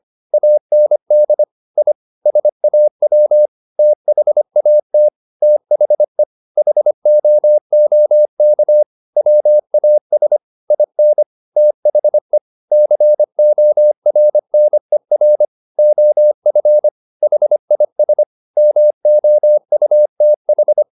Morse code, heard as a keyed tone, about 25 wpm, 600 Hz. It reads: AND I SAW THAT THE HOOK WAS IN THE CORNER OF HIS MOUTH